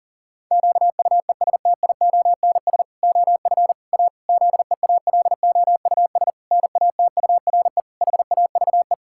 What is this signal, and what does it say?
Morse code, heard as a keyed tone, about 37 wpm, 700 Hz. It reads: QUESTIONS OF A ZEALOUS NATURE HAVE
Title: Pangram Challenge v2 - 40wpm